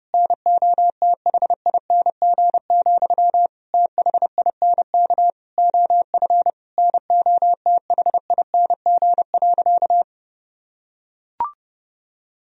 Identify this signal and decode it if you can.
Morse code, heard as a keyed tone, about 30 wpm, 700 Hz. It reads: NOTHING, THINK OF NOTHING.
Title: Sentences from Top 1000 Words - Review Part 3 - 30wpm